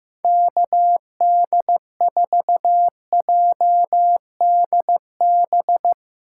Morse code, heard as a keyed tone, about 15 wpm, 700 Hz. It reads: KD4JDB
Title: USA Callsign DE Callsign 15wpm